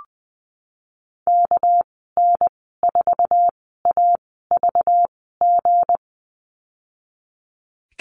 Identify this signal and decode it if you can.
Morse code, heard as a keyed tone, about 20 wpm, spaced as for 15 wpm, 700 Hz. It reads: KN4AVG
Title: 207 Lesson 33g - Letter j - Call Signs 20wpm